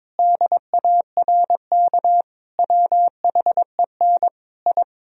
Morse code, heard as a keyed tone, about 22 wpm, 700 Hz. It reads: DARK WHEN I
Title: Sentences from Top 1000 Words - Review Part 2 - 22wpm